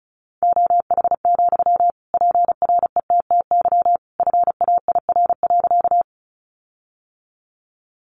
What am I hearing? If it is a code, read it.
Morse code, heard as a keyed tone, about 34 wpm, 700 Hz. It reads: OH, PRETTY FAIR.
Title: Sentences from Top 1300 Words 35wpm